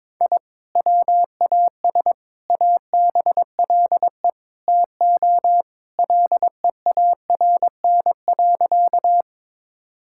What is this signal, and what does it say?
Morse code, heard as a keyed tone, about 21 wpm, 700 Hz. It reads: I WAS ABLE TO LEARN.
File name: practice / 22wpm / Sentences with 4-Word Phrases - 22wpm